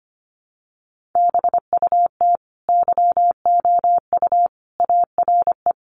Morse code, heard as a keyed tone, about 25 wpm, 700 Hz. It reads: BUT YOU ARE
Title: Sentences from Top 1200 Words 25wpm